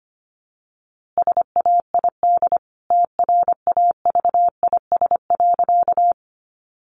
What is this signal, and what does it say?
Morse code, heard as a keyed tone, about 25 wpm, 700 Hz. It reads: SAID TRAVIS.
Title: A Sound Of Thunder 25wpm